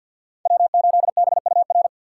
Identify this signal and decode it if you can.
Morse code, heard as a keyed tone, about 53 wpm, 700 Hz. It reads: W8BUR
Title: USA Call Signs - 5 Characters - 6x Character Spacing - 50wpm